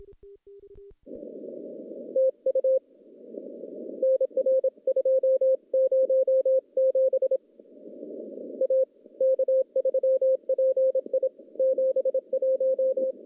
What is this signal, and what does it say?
Morse code, heard as a keyed tone, about 27 wpm, 530 Hz. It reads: TU NR207 A K3PI 71
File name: CW-SS-k5tr